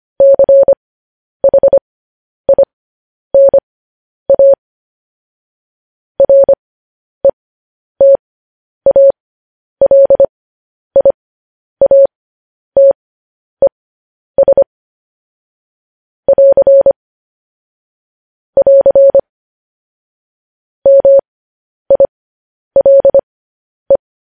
Morse code, heard as a keyed tone, about 25 wpm, spaced as for 10 wpm, 550 Hz. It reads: CHINA RETALIATES <AR> <AR> MILE
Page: news headlines in morse code at 10 wpm